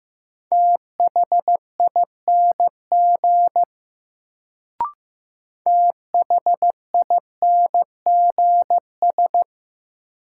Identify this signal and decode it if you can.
Morse code, heard as a keyed tone, about 15 wpm, 700 Hz. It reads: THING THINGS
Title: Sets of 3 Words - New Word Drill - 15wpm